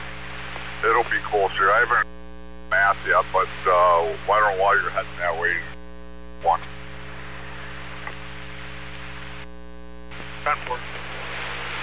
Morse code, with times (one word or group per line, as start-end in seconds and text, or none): none